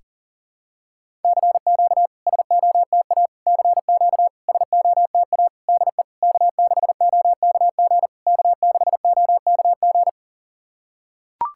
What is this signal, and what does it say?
Morse code, heard as a keyed tone, about 40 wpm, 690 Hz. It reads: CQ SOTA CQ SOTA DE K6OKG K6OKG
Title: SOTA QSO Practice 40wpm